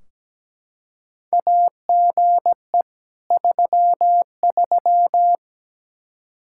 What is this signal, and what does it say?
Morse code, heard as a keyed tone, about 17 wpm, 700 Hz. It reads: AGE 33
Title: Practice QSO - Exchange 3 - More Info - 17wpm